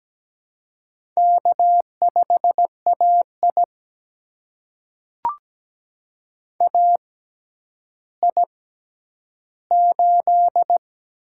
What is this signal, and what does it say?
Morse code, heard as a keyed tone, about 17 wpm, spaced as for 10 wpm, 700 Hz. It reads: K5AI AI8
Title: USA Call Signs - 4 Characters - 6x Character Spacing - 17wpm